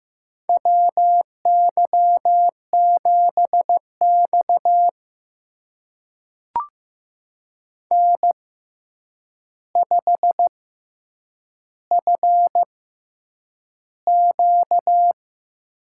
Morse code, heard as a keyed tone, about 15 wpm, spaced as for 8 wpm, 700 Hz. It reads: WY7X N5FQ
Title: USA Call Signs - 4 Characters - 6x Character Spacing - 15wpm